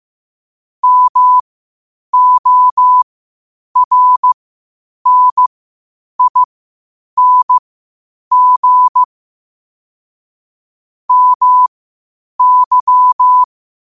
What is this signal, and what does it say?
Morse code, heard as a keyed tone, about 15 wpm, spaced as for 8 wpm, 1000 Hz. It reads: MORNING MY
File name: qotdfiles / 8wpm